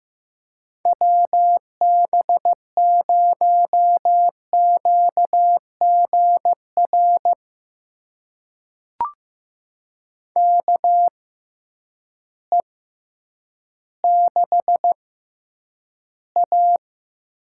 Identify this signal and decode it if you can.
Morse code, heard as a keyed tone, about 15 wpm, spaced as for 9 wpm, 700 Hz. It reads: WB0QGR KE6A
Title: USA Call Signs - 6 Characters - 6x Character Spacing - 15wpm